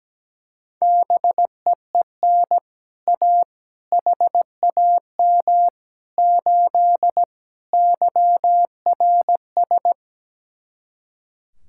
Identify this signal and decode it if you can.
Morse code, heard as a keyed tone, about 17 wpm, 700 Hz. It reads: BEEN A HAM 8 YRS
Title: Speed-Racing Common QSO Phrases Encore - 17wpm